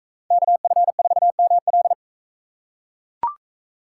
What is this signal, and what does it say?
Morse code, heard as a keyed tone, about 44 wpm, 700 Hz. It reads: KF4ML